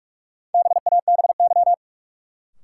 Morse code, heard as a keyed tone, about 46 wpm, 700 Hz. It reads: BABY
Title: Sentences from Top 1300 Words - New Word Drill - 45wpm